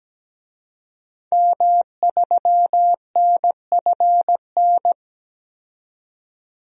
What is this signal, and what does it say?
Morse code, heard as a keyed tone, about 17 wpm, 700 Hz. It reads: M3NFN
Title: DXCC-100 Call Signs - 5 Characters - 17wpm